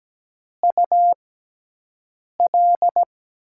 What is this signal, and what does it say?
Morse code, heard as a keyed tone, about 17 wpm, spaced as for 6 wpm, 700 Hz. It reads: UL
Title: USA Call Signs - 5 Characters - 6x Character Spacing - 17wpm